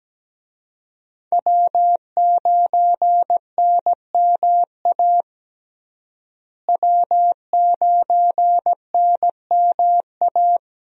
Morse code, heard as a keyed tone, about 17 wpm, 700 Hz. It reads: W9NMA W9NMA
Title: USA Call Signs - 5 Characters - Repeated 2x - 17wpm